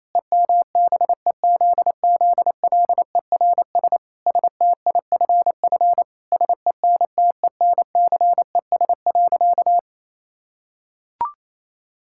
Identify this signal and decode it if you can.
Morse code, heard as a keyed tone, about 28 wpm, 700 Hz. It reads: EMBEZZLERS STIFF SENTENCES.
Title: Pangram Challenge v2 - 28wpm